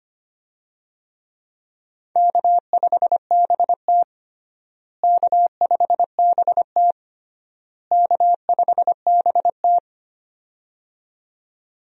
Morse code, heard as a keyed tone, about 25 wpm, 700 Hz. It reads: K5BT K5BT K5BT
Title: USA Call Signs - 4 Characters - Repeated 3x - 25wpm